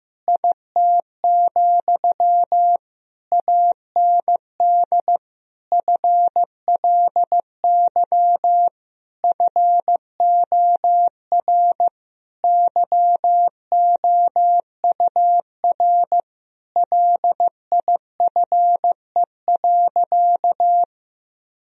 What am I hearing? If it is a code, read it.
Morse code, heard as a keyed tone, about 15 wpm, 700 Hz. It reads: IT, AND FLY FOR YOUR LIFE.
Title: Sentences from Top 1000 Words - Review Part 2 - 15wpm